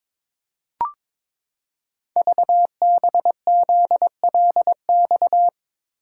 Morse code, heard as a keyed tone, about 22 wpm, 700 Hz. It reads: VBZLX
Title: Code Groups - 5 Letters - 22wpm